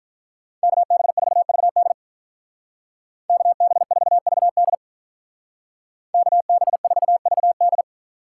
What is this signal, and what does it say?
Morse code, heard as a keyed tone, about 43 wpm, 700 Hz. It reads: KB4VD KB4VD KB4VD
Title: Speed Racing - USA 2x2 Call Signs 35wpm